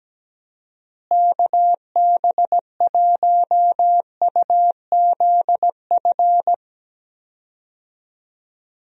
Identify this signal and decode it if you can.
Morse code, heard as a keyed tone, about 17 wpm, 700 Hz. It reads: KB1UZF
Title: USA Call Signs - 6 Characters - Repeated 2x - 17wpm